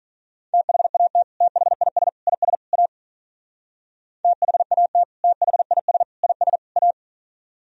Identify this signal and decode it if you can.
Morse code, heard as a keyed tone, about 41 wpm, 700 Hz. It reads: THAT THIS IS A THAT THIS IS A
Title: Speed-Racing - Sets of 4 Words 35wpm